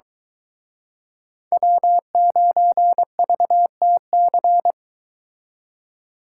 Morse code, heard as a keyed tone, about 23 wpm, 700 Hz. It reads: W9VTC